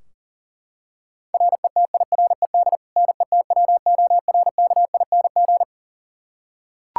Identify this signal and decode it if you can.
Morse code, heard as a keyed tone, about 40 wpm, 700 Hz. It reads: RETIRED NETWORKING
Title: K1USN SST QSO Practice 40wpm